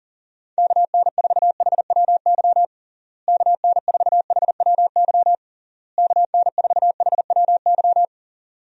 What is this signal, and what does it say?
Morse code, heard as a keyed tone, about 40 wpm, 700 Hz. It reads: KN4HWY KN4HWY KN4HWY
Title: USA Call Signs - 6 Characters - Repeated 3x - 40wpm